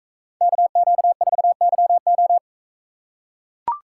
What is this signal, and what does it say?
Morse code, heard as a keyed tone, about 40 wpm, 700 Hz. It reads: KQ4YO